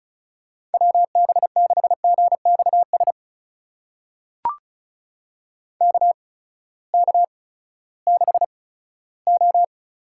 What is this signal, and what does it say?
Morse code, heard as a keyed tone, about 36 wpm, spaced as for 18 wpm, 700 Hz. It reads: WB6GXS KK6O
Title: USA Call Signs - 6 Characters - 8x Character Spacing - 35wpm